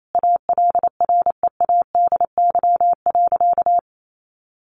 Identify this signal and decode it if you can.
Morse code, heard as a keyed tone, about 28 wpm, 700 Hz. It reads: ALREADY.
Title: Sentences from Top 900 Words 28wpm